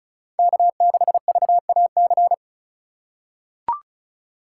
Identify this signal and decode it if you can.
Morse code, heard as a keyed tone, about 36 wpm, 690 Hz. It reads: K6VAC
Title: USA Call Signs - 4x Character Spacing - 35wpm